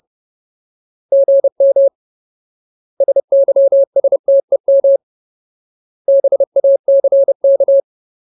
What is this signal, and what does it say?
Morse code, heard as a keyed tone, about 30 wpm, 550 Hz. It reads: GM SYSTEM BACK
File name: Ultimate Word List / 30WPM Ultimate